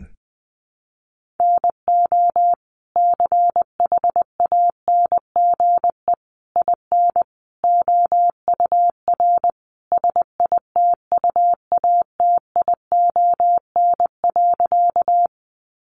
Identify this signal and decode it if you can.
Morse code, heard as a keyed tone, about 20 wpm, 700 Hz. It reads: NO CHANGE IN OUR SITUATION.
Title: Sentences from Top 1200 Words 20wpm